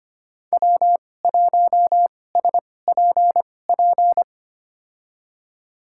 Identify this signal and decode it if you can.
Morse code, heard as a keyed tone, about 25 wpm, spaced as for 18 wpm, 700 Hz. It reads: W1SPP
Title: USA Call Signs - 5 Characters - 2x Character Spacing - 25wpm